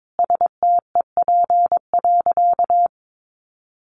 Morse code, heard as a keyed tone, about 22 wpm, 700 Hz. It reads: STEP.
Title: Sentences from Top 1400 Words 22wpm